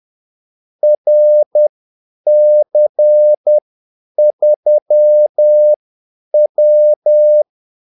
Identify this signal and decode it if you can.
Morse code, heard as a keyed tone, about 10 wpm, spaced as for 8 wpm, 600 Hz. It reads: RC3W
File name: Callsigns-10-8wpm